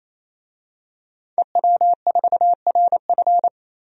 Morse code, heard as a keyed tone, about 28 wpm, 700 Hz. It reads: EW4RF
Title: DXCC-100 Call Signs - 5 Characters - 28wpm